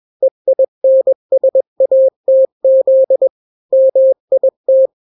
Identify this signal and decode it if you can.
Morse code, heard as a keyed tone, about 21 wpm, 530 Hz. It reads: EINSATZ MIT
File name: daily / 20wpm